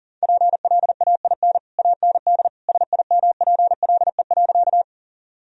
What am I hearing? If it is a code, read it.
Morse code, heard as a keyed tone, about 40 wpm, 700 Hz. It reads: PLAIN AND SIMPLE.
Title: Sentences from Top 1000 Words - Review Part 2 - 40wpm